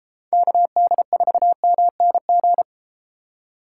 Morse code, heard as a keyed tone, about 32 wpm, 700 Hz. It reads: KD4MNG